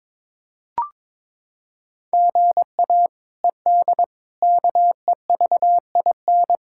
Morse code, heard as a keyed tone, about 21 wpm, 700 Hz. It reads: GA ED KEVIN
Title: K1USN SST QSO Practice - 22wpm